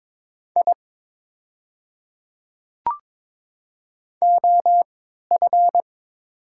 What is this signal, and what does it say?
Morse code, heard as a keyed tone, about 22 wpm, spaced as for 12 wpm, 700 Hz. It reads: I OF
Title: 3x Character Spacing - Top 200 Words - 22wpm